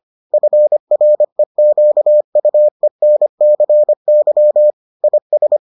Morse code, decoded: FREQUENCY IS

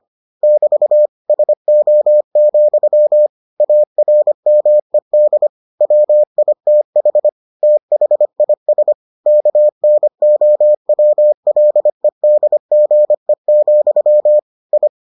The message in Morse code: <BT> SO, ARMED WITH THIS KNOWLEDGE, I

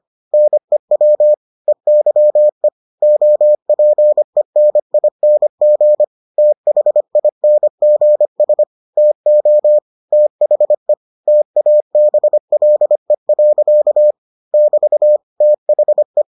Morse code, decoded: NEW EYE OPENING THINGS TO THE TABLE. <BT> THE

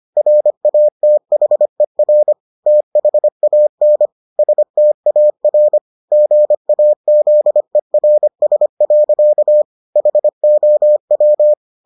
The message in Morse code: RATHER THAN STAR GAZERS. HOW

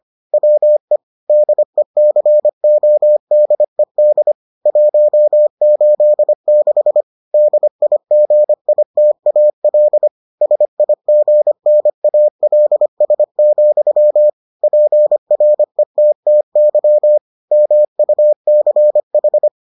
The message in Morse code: WE DECODED 186 DIGITAL SIGNALS, PRETTY MUCH

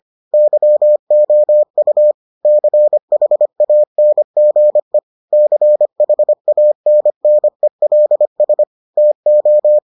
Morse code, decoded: YOU CHANGE CHANNELS TO

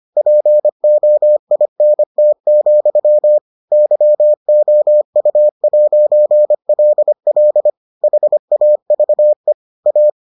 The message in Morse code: POINT, YOU'LL HAVE A

